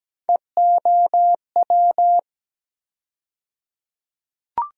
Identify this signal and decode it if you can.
Morse code, heard as a keyed tone, about 17 wpm, 700 Hz. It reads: EOW
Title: USA Call Sign Postfixes - 3 Letters 17wpm